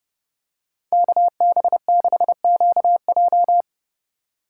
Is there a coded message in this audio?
Yes: Morse code from 0.9 to 3.6 s, heard as a keyed tone, about 31 wpm, 700 Hz. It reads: KB6QJ